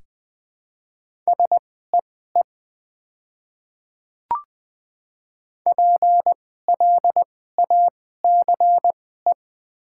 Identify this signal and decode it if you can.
Morse code, heard as a keyed tone, about 20 wpm, spaced as for 14 wpm, 700 Hz. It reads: SEE PLACE
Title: 2x Character Spacing - Top 300 Words - 20wpm